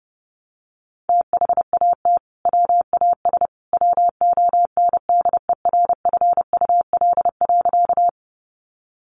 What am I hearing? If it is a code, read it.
Morse code, heard as a keyed tone, about 28 wpm, 700 Hz. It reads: THAT WAS WONDERFUL.
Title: Sentences from Top 1100 Words 30wpm